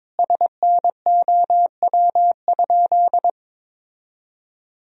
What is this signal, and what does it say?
Morse code, heard as a keyed tone, about 22 wpm, 700 Hz. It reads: SNOW?
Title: Sentences from Top 1000 Words - Review Part 3 - 22wpm